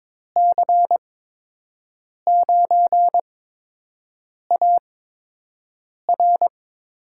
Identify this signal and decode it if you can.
Morse code, heard as a keyed tone, about 21 wpm, spaced as for 6 wpm, 700 Hz. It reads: C9AR
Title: USA Call Signs - 6 Characters - 8x Character Spacing - 22wpm